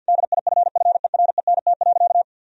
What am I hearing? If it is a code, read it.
Morse code, heard as a keyed tone, about 50 wpm, 700 Hz. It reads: DIFFERENT.
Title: Sentences from Top 1000 Words - Review Part 3 - 50wpm